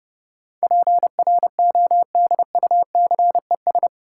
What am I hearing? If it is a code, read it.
Morse code, heard as a keyed tone, about 30 wpm, 700 Hz. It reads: PRODUCES
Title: Sentences from Top 1000 Words - Review Part 6 - New Word Drill - 30wpm